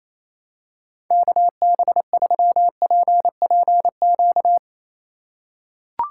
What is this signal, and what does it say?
Morse code, heard as a keyed tone, about 28 wpm, 700 Hz. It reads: KB3PPQ
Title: USA Call Signs - 6 Characters - 4x Character Spacing - 28wpm